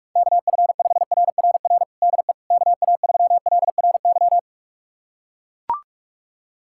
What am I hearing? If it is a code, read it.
Morse code, heard as a keyed tone, about 41 wpm, 690 Hz. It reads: KF5RRR DE KA3LRY
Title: USA Callsign DE Callsign - 6 Characters - 45wpm